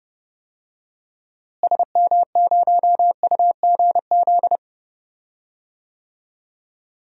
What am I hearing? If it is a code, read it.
Morse code, heard as a keyed tone, about 30 wpm, 700 Hz. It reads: SM0UGZ
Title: DXCC-100 Call Signs - 6 Characters - Repeated 3x - 30wpm